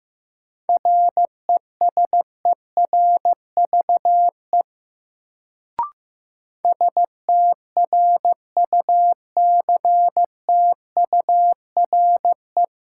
Morse code, heard as a keyed tone, about 15 wpm, 700 Hz. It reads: RESERVE STRUCTURE
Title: Sentences from Top 1000 Words - Review Part 6 - New Word Drill - 15wpm